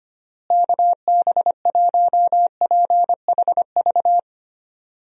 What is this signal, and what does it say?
Morse code, heard as a keyed tone, about 25 wpm, 700 Hz. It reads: KB1PHV
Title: USA Call Signs - 6 Characters - Repeated 2x - 25wpm